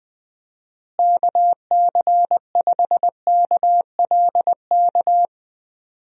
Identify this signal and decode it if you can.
Morse code, heard as a keyed tone, about 20 wpm, 700 Hz. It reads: KC5KLK
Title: USA Call Signs - 6 Characters - Repeated 2x - 20wpm